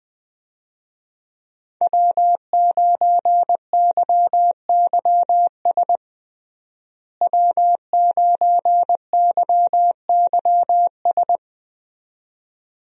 Morse code, heard as a keyed tone, about 20 wpm, 700 Hz. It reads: W9YYS W9YYS
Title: USA Call Signs - 5 Characters - Repeated 2x - 20wpm